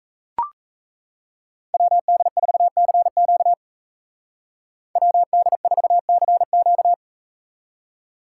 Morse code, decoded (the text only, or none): WD4CQ WD4CQ